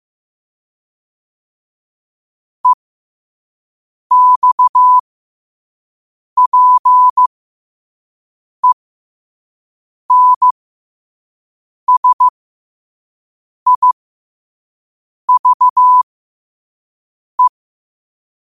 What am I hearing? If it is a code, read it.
Morse code, heard as a keyed tone, about 15 wpm, spaced as for 5 wpm, 1000 Hz. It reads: EXPENSIVE
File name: qotdfiles / 5wpm